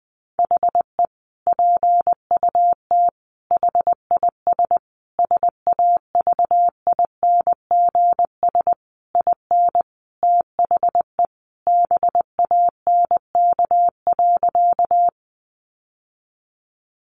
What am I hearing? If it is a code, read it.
Morse code, heard as a keyed tone, about 20 wpm, 700 Hz. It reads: HE PUT HIS SAVINGS IN THE BANK.
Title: Sentences from Top 1200 Words 20wpm